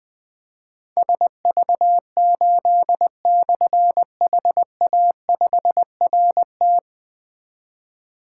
Morse code, heard as a keyed tone, about 20 wpm, 700 Hz. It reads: SV8/HA5RT
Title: Difficult Call Signs 20wpm